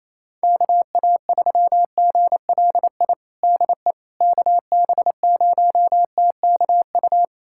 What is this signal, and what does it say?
Morse code, heard as a keyed tone, about 28 wpm, 700 Hz. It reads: KA3GLI DE KB0TKU
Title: USA Callsign DE Callsign - 6 Characters - 28wpm